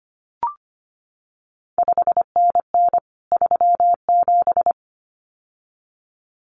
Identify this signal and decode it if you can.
Morse code, heard as a keyed tone, about 25 wpm, 700 Hz. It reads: 5NN 37
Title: CQ WW DX Contest Exchange Encore - 25wpm